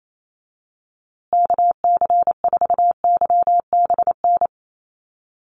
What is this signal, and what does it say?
Morse code, heard as a keyed tone, about 28 wpm, 700 Hz. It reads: KC4YBN
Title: USA Call Signs Repeated 3x 28wpm